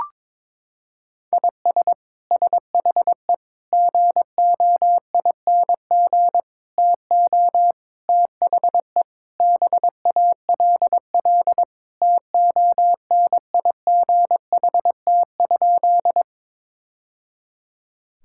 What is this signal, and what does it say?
Morse code, heard as a keyed tone, about 22 wpm, 700 Hz. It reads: IS SHE GOING TO THE BALL TONIGHT?
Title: Sentences from Top 1000 Words - Review Part 6 - 22wpm